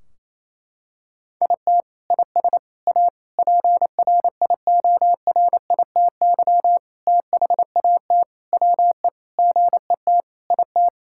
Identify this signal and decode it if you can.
Morse code, heard as a keyed tone, about 28 wpm, 700 Hz. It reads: IT IS A PRIORITY THAT WE GET IT
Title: Sentences from Top 1600 Words 28wpm